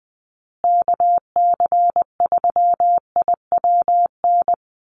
Morse code, heard as a keyed tone, about 20 wpm, 700 Hz. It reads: KC3IWN
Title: USA Call Signs Repeated 2x 20wpm